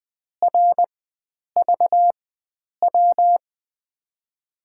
Morse code, heard as a keyed tone, about 20 wpm, spaced as for 9 wpm, 700 Hz. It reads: RVW
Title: USA Call Signs - 6 Characters - 4x Character Spacing - 20wpm